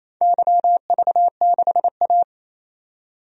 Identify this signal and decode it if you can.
Morse code, heard as a keyed tone, about 28 wpm, 700 Hz. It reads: YV6A
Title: DXCC-100 Call Signs - 4 Characters - 28wpm